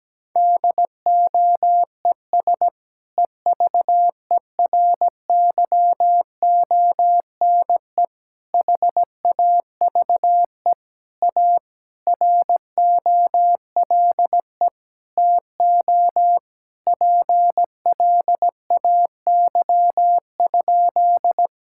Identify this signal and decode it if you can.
Morse code, heard as a keyed tone, about 17 wpm, 700 Hz. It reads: DOES EVERYONE HAVE A ROLE TO PLAY?
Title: Sentences from Top 1000 Words - Review Part 6 - 17wpm